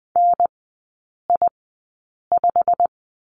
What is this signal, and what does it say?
Morse code, heard as a keyed tone, about 20 wpm, spaced as for 8 wpm, 700 Hz. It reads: NI5
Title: 152 Lesson 25e - Letter p - Three Characters 20wpm